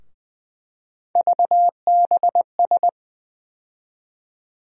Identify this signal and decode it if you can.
Morse code, heard as a keyed tone, about 20 wpm, 700 Hz. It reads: VBS
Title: USA Call Sign Postfixes - 3 Letters 20wpm